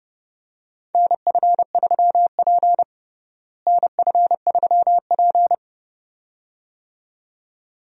Morse code, heard as a keyed tone, about 30 wpm, 700 Hz. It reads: NF3P NF3P
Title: USA Call Signs - 4 Characters - Repeated 2x - 30wpm